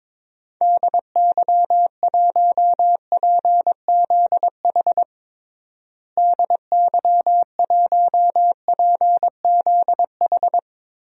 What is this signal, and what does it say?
Morse code, heard as a keyed tone, about 22 wpm, 700 Hz. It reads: DY1PZH DY1PZH
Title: DXCC-100 Call Signs - 6 Characters - Repeated 3x - 22wpm